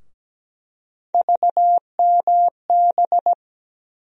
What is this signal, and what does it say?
Morse code, heard as a keyed tone, about 17 wpm, 700 Hz. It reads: VMB
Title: USA Call Sign Postfixes - 3 Letters 17wpm